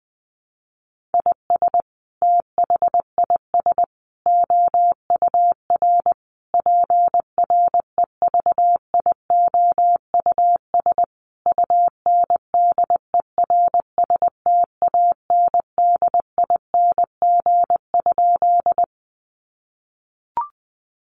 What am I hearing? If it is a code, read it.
Morse code, heard as a keyed tone, about 20 wpm, 700 Hz. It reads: IS THIS OUR PREVIOUS UNDERSTANDING?
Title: Sentences from Top 1500 Words 20wpm